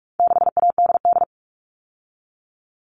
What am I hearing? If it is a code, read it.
Morse code, heard as a keyed tone, about 44 wpm, 700 Hz. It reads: 6ADD
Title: All Cut Numbers - 4-Digit 45wpm